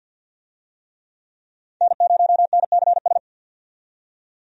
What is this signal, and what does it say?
Morse code, heard as a keyed tone, about 46 wpm, 700 Hz. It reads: N0NCS
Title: USA Call Signs - 5 Characters - 50wpm